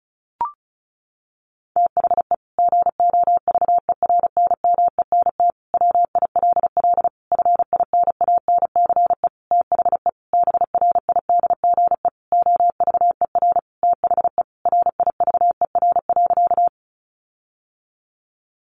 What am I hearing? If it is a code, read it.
Morse code, heard as a keyed tone, about 34 wpm, 700 Hz. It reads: THE GOVERNMENT WILL FINANCE THE BRIDGE OVER THE RIVER.
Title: Sentences from Top 1500 Words 35wpm